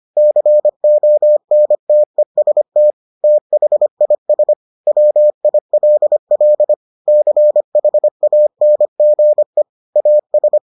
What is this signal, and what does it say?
Morse code, heard as a keyed tone, about 25 wpm, 600 Hz. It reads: CONTEST THIS WILL CHANGE AS